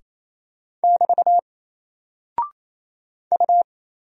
Morse code, heard as a keyed tone, about 29 wpm, 700 Hz. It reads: <BT> U
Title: Single Character (Letter:Number:Punctuation:Prosign) Alt 28wpm